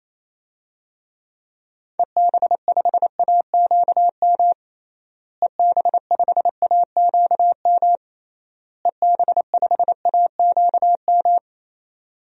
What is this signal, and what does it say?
Morse code, heard as a keyed tone, about 28 wpm, 700 Hz. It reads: EB5AQM EB5AQM EB5AQM
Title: DXCC-100 Call Signs - 6 Characters - Repeated 3x - 28wpm